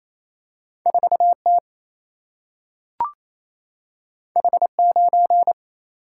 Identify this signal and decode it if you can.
Morse code, heard as a keyed tone, about 28 wpm, 700 Hz. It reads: 4 T H 9
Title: Single-Letter Single-Number 28wpm